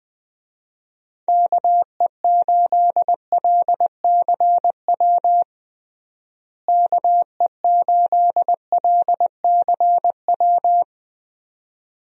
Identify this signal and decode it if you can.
Morse code, heard as a keyed tone, about 20 wpm, 700 Hz. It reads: KE8LCW KE8LCW
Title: USA Call Signs - 6 Characters - Repeated 2x - 20wpm